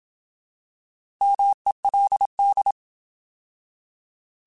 Morse code, heard as a keyed tone, about 27 wpm, 800 Hz. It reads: MELD